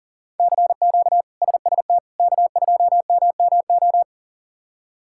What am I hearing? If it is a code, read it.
Morse code, heard as a keyed tone, about 37 wpm, 700 Hz. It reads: CQ SST K2MMO
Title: K1USN SST QSO Practice - 40wpm